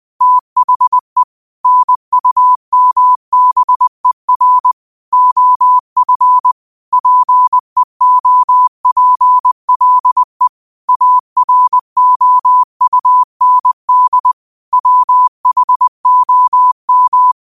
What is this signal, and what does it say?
Morse code, heard as a keyed tone, about 20 wpm, 1000 Hz. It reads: THE NUMBER OF PEOPLE AROUND WHOM